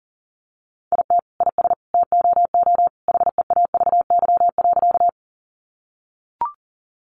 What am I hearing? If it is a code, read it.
Morse code, heard as a keyed tone, about 40 wpm, 700 Hz. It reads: IT IS TOO HEAVY.
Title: Sentences from Top 1100 Words 40wpm